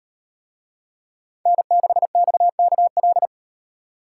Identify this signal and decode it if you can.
Morse code, heard as a keyed tone, about 38 wpm, 700 Hz. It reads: N6XKL